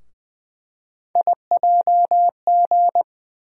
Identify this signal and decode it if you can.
Morse code, heard as a keyed tone, about 20 wpm, 700 Hz. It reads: IJG
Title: USA Call Sign Postfixes - 3 Letters 20wpm